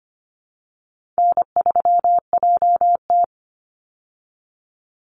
Morse code, heard as a keyed tone, about 25 wpm, 700 Hz. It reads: N3JT
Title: Easy Call Signs Repeated 3x - 25wpm